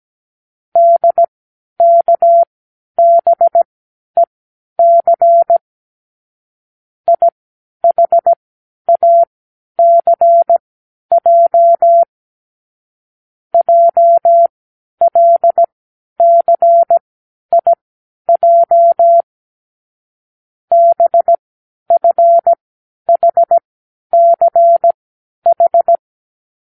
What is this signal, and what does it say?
Morse code, heard as a keyed tone, about 17 wpm, spaced as for 10 wpm, 680 Hz. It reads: DKBEC IHACJ JLCIJ BFHCH
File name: k7qo / Lesson18(A-L)